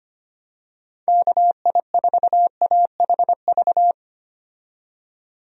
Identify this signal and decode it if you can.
Morse code, heard as a keyed tone, about 25 wpm, 700 Hz. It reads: KI4AHV
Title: USA Call Signs - 6 Characters - 25wpm